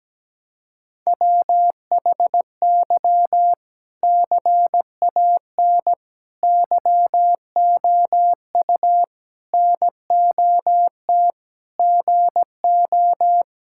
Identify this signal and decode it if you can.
Morse code, heard as a keyed tone, about 17 wpm, 700 Hz. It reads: WHY CAN YOU NOT GO
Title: Sentences from Top 1600 Words 17wpm